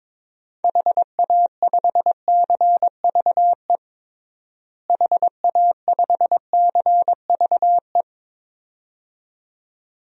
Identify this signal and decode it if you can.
Morse code, heard as a keyed tone, about 22 wpm, 700 Hz. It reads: HA5CVE HA5CVE
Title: DXCC-100 Call Signs - 6 Characters - Repeated 2x - 22wpm